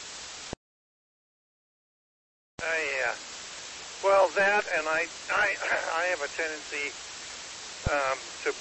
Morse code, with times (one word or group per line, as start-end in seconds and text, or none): none